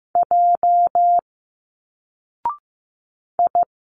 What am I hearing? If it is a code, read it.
Morse code, heard as a keyed tone, about 15 wpm, 700 Hz. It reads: J I
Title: Single Letters 15wpm